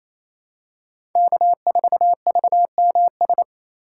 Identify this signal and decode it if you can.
Morse code, heard as a keyed tone, about 28 wpm, 700 Hz. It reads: K4VMS
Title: USA Call Signs - 5 Characters - 4x Character Spacing - 28wpm